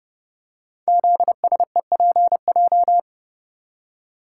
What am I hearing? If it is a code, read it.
Morse code, heard as a keyed tone, about 29 wpm, 700 Hz. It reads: ZSEPJ
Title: Code Groups - 5 Letters - 30wpm